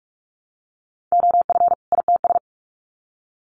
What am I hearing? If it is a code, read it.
Morse code, heard as a keyed tone, about 44 wpm, 700 Hz. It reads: OF ITS
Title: Sets of 2 Words 45wpm